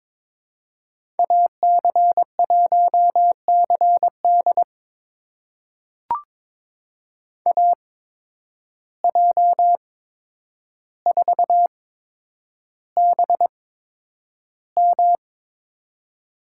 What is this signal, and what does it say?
Morse code, heard as a keyed tone, about 23 wpm, spaced as for 10 wpm, 700 Hz. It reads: AC1CD AJ4BM
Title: USA Call Signs - 8x Character Spacing - 22wpm